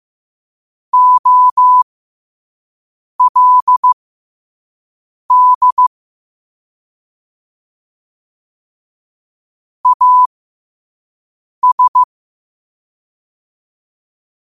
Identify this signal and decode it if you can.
Morse code, heard as a keyed tone, about 15 wpm, spaced as for 5 wpm, 1000 Hz. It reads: OLD AS
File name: qotdfiles / 5wpm